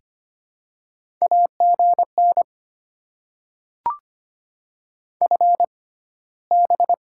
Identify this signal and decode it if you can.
Morse code, heard as a keyed tone, about 25 wpm, spaced as for 15 wpm, 700 Hz. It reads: AGN FB
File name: Top 20 QSO Elements - 6x Character Spacing - 25wpm